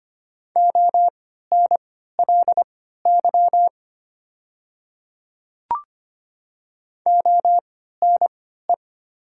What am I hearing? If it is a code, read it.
Morse code, heard as a keyed tone, about 25 wpm, spaced as for 14 wpm, 700 Hz. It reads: ONLY ONE
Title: 3x Character Spacing - Top 200 Words - 25wpm